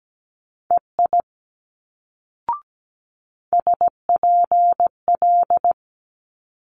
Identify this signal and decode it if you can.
Morse code, heard as a keyed tone, about 17 wpm, 700 Hz. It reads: EI SPL
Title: Multi-letter Phonemes Quick Fire 17wpm